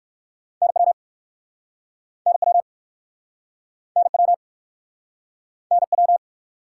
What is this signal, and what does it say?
Morse code, heard as a keyed tone, about 59 wpm, 700 Hz. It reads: NW NW NW NW